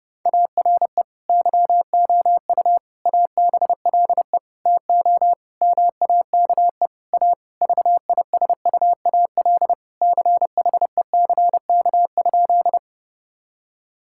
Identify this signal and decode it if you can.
Morse code, heard as a keyed tone, about 30 wpm, 700 Hz. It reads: ARE YOU ABLE TO MAKE A VISUAL CHECK?
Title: Sentences from Top 1000 Words - Review Part 6 - 30wpm